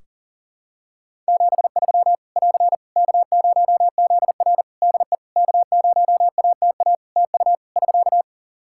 Morse code, heard as a keyed tone, about 43 wpm, 700 Hz. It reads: 73 <AR> K0ZR DE K0ATA TU <SK>
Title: Practice QSO - Exchange 4 - Ending - 40wpm